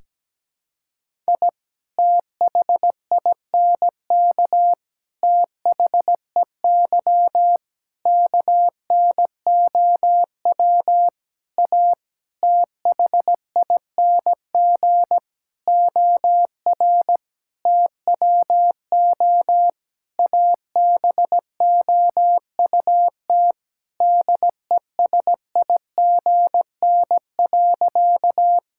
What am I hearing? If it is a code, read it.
Morse code, heard as a keyed tone, about 17 wpm, 700 Hz. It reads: I THINK THEY KNOW A THING OR TWO ABOUT DESIGN.
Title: Sentences from Top 1000 Words - Review Part 5 - 17wpm